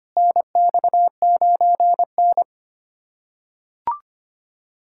Morse code, heard as a keyed tone, about 25 wpm, 700 Hz. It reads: NX9N
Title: USA Callsign DE Callsign - 4 Characters - 25wpm